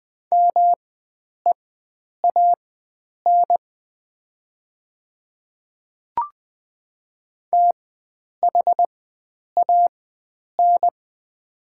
Morse code, heard as a keyed tone, about 21 wpm, spaced as for 9 wpm, 700 Hz. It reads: MEAN THAN
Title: 4x Character Spacing - Top 300 Words - 20wpm